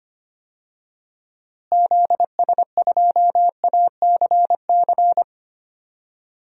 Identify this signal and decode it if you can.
Morse code, heard as a keyed tone, about 25 wpm, 700 Hz. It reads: ZS2ACC
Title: DXCC-100 Call Signs - 6 Characters - Repeated 3x - 25wpm